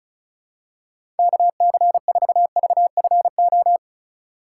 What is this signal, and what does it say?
Morse code, heard as a keyed tone, about 36 wpm, 700 Hz. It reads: KC4VFO
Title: USA Call Signs - 6 Characters - 4x Character Spacing - 35wpm